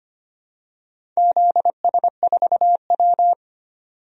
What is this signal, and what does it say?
Morse code, heard as a keyed tone, about 25 wpm, 700 Hz. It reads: ZS4W
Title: DXCC-100 Call Signs - 4 Characters - 25wpm